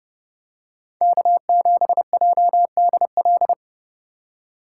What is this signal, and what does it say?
Morse code, heard as a keyed tone, about 29 wpm, 700 Hz. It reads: K7JDL